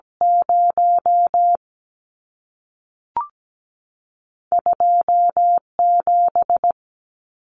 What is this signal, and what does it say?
Morse code, heard as a keyed tone, about 17 wpm, 700 Hz. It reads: 0 2 7
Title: Numbers - 2-Digit 17wpm